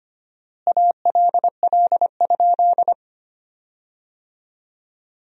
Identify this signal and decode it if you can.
Morse code, heard as a keyed tone, about 25 wpm, 700 Hz. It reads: ALL?
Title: 4x Word-Spacing - Sentences from Top 100 Words - 25wpm